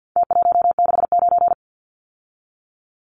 Morse code, heard as a keyed tone, about 50 wpm, 700 Hz. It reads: T169
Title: Numbers - Contest Style - 4-Digit 50wpm